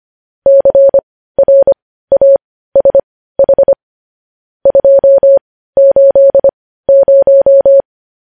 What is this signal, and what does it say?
Morse code, heard as a keyed tone, about 25 wpm, spaced as for 15 wpm, 550 Hz. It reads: CRASH 280
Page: news-headlines-in-morse-code-at-15-wpm